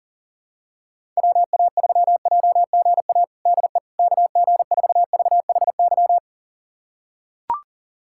Morse code, heard as a keyed tone, about 40 wpm, 700 Hz. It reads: WA3JGA DE KG4VHY
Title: USA Callsign DE Callsign 40wpm